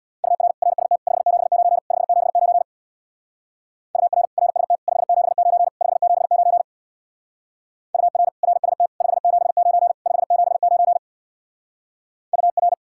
Morse code, heard as a keyed tone, about 62 wpm, 700 Hz. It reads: UR RST 579 579 UR RST 579 579 UR RST 579 579 UR
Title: Speed-Racing Common QSO Phrases - 50wpm